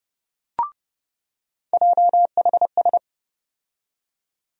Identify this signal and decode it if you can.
Morse code, heard as a keyed tone, about 30 wpm, 700 Hz. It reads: JHS
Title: USA Call Sign Postfixes - 3 Letters 30wpm